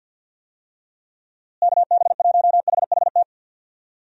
Morse code, heard as a keyed tone, about 52 wpm, 700 Hz. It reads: KB1HHT
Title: USA Call Signs - 6 Characters - Repeated 3x - 50wpm